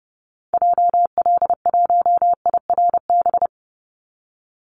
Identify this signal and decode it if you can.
Morse code, heard as a keyed tone, about 30 wpm, 700 Hz. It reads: JL1IRB
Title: Call Signs Repeated 3x - 30wpm